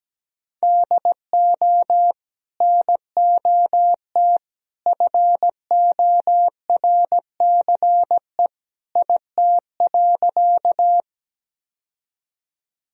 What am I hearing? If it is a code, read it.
Morse code, heard as a keyed tone, about 17 wpm, 700 Hz. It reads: DO NOT FORCE IT.
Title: Sentences from Top 1000 Words - Review Part 3 - 17wpm